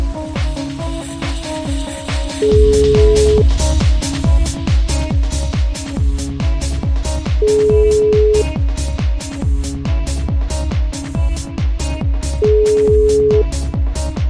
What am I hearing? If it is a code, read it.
Ringback tone (Europe, 425 Hz), with noise.